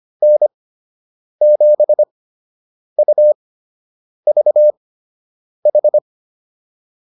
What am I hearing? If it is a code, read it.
Morse code, heard as a keyed tone, about 25 wpm, spaced as for 8 wpm, 600 Hz. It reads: N7UVH